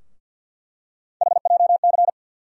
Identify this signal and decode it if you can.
Morse code, heard as a keyed tone, about 48 wpm, 700 Hz. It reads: HJC
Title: USA Call Sign Postfixes - 3 Letters 50wpm